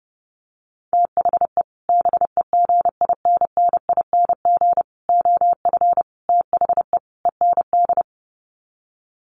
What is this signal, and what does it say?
Morse code, heard as a keyed tone, about 30 wpm, 700 Hz. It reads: THE BEGINNING OF THE END
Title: Sets of 5 Words Encore2 30wpm